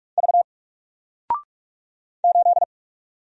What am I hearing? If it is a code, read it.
Morse code, heard as a keyed tone, about 44 wpm, 700 Hz. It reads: V 8
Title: Single Character (Letter:Number:Punctuation:Prosign) 45wpm